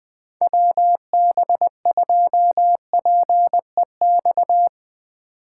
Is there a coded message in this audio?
Yes: Morse code from 0.4 to 4.7 s, heard as a keyed tone, about 19 wpm, 700 Hz. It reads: WB2PEX